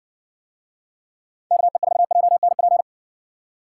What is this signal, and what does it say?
Morse code, heard as a keyed tone, about 61 wpm, 700 Hz. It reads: KE4JNP